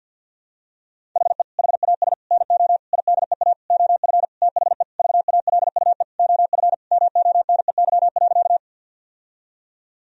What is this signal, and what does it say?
Morse code, heard as a keyed tone, about 50 wpm, 700 Hz. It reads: HE HAS NO IDEA OF THE VALUE OF MONEY.
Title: Sentences from Top 1000 Words - Review Part 3 - 50wpm